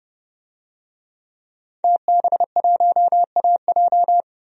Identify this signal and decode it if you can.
Morse code, heard as a keyed tone, about 30 wpm, 700 Hz. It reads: TB1AJ
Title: DXCC-100 Call Signs - 5 Characters - 30wpm